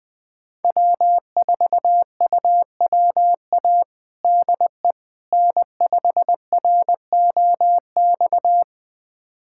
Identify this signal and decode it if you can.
Morse code, heard as a keyed tone, about 20 wpm, 700 Hz. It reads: W4UWA DE N5ROX
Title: USA Callsign DE Callsign - 5 Characters - 20wpm